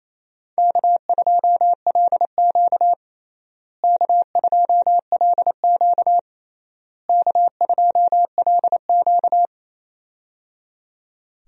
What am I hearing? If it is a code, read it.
Morse code, heard as a keyed tone, about 28 wpm, 700 Hz. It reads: K2LQ K2LQ K2LQ
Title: USA Call Signs - 4 Characters - Repeated 3x - 28wpm